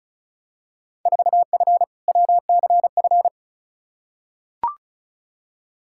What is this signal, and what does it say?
Morse code, heard as a keyed tone, about 36 wpm, 700 Hz. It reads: 4F WCF
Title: Field Day Exchanges - 35wpm